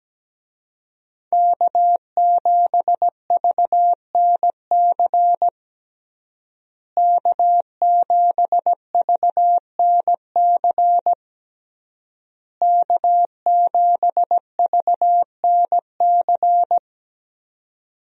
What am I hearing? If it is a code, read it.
Morse code, heard as a keyed tone, about 17 wpm, 700 Hz. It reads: K7VNC K7VNC K7VNC
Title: USA Call Signs - 5 Characters - Repeated 3x - 17wpm